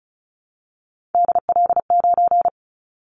Morse code, heard as a keyed tone, about 36 wpm, 690 Hz. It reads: DL9
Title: Callsign Prefixes for Top 50 DXCC - 35wpm